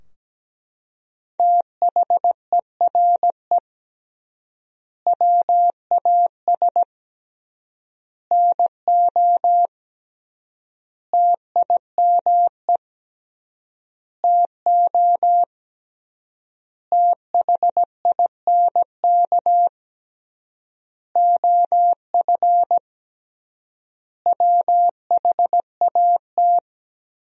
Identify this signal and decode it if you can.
Morse code, heard as a keyed tone, about 17 wpm, 700 Hz. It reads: THERE WAS NO TIME TO THINK OF WHAT
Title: 3x Word-Spacing - Sentences from Top 400 Words - 17wpm